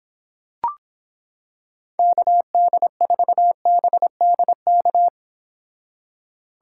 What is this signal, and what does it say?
Morse code, heard as a keyed tone, about 26 wpm, 700 Hz. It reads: KD4BDK